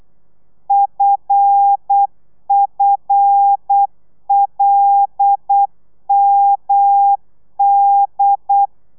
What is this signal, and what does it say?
Morse code, heard as a keyed tone, about 8 wpm, 800 Hz. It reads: FFLMD
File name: telegrafia / i2tfj / Lez5 35